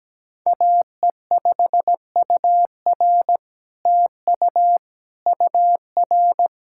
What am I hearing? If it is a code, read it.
Morse code, heard as a keyed tone, about 17 wpm, 700 Hz. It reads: AE5UR TU UR
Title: SOTA QSO Practice 17wpm